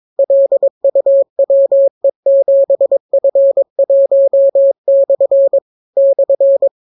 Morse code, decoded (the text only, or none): LUWE7F1/ /